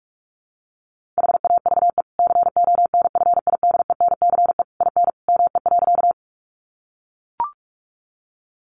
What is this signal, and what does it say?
Morse code, heard as a keyed tone, about 44 wpm, 700 Hz. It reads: HAVE CONFIDENCE IN ME.
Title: Sentences from Top 1500 Words 45wpm